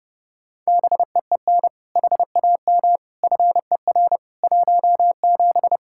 Morse code, heard as a keyed tone, about 30 wpm, 700 Hz. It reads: BEEN HAM FER 17
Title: Practice QSO - Exchange 3 - More Info - 30wpm